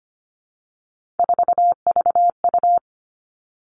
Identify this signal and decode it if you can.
Morse code, heard as a keyed tone, about 25 wpm, 700 Hz. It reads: 4VU
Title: Three Challenging Letter-Number Combinations 25wpm